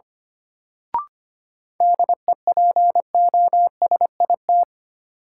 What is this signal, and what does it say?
Morse code, heard as a keyed tone, about 25 wpm, 700 Hz. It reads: DEPOSIT
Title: Sentences from Top 1300 Words - New Word Drill - 25wpm